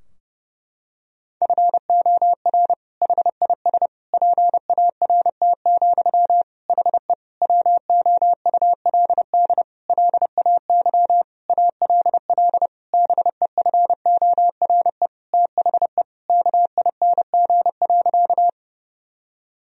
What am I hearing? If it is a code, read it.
Morse code, heard as a keyed tone, about 30 wpm, 700 Hz. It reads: FOR HIS PART, HE WOULD LAY ALL BEFORE THE KING.
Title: Sentences from Top 1000 Words - Review Part 2 - 30wpm